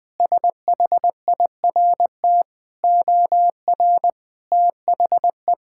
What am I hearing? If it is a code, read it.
Morse code, heard as a keyed tone, about 20 wpm, 700 Hz. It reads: SHIRT OR THE
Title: Sentences with 2-Word Phrases - 20wpm